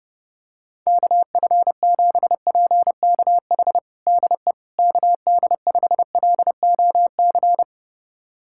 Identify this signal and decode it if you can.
Morse code, heard as a keyed tone, about 30 wpm, 700 Hz. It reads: KF7PKH DE KD5LOC
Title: USA Callsign DE Callsign 30wpm